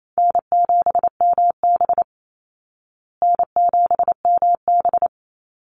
Morse code, heard as a keyed tone, about 28 wpm, 700 Hz. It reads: N7MB N7MB
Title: Easy Call Signs Repeated 3x - 28wpm